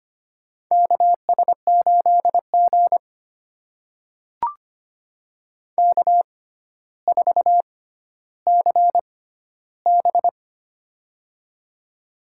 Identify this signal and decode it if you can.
Morse code, heard as a keyed tone, about 25 wpm, spaced as for 13 wpm, 700 Hz. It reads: KS8G K4CB
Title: USA Call Signs - 4 Characters - 6x Character Spacing - 25wpm